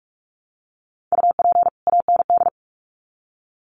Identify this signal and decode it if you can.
Morse code, heard as a keyed tone, about 44 wpm, 700 Hz. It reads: UP AND
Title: Sets of 2 Words 45wpm